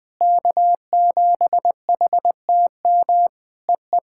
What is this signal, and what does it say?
Morse code, heard as a keyed tone, about 20 wpm, 700 Hz. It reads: K7HTM EE
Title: POTA QSO Practice - 20wpm